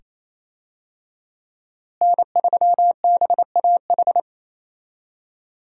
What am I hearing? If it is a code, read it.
Morse code, heard as a keyed tone, about 27 wpm, 700 Hz. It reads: N3BAH
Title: USA Call Signs - 5 Characters - Repeated 2x - 28wpm